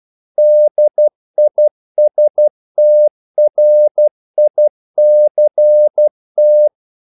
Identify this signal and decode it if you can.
Morse code, heard as a keyed tone, about 12 wpm, 600 Hz. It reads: DISTRICT
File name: CW / Practice / NationalCapitals12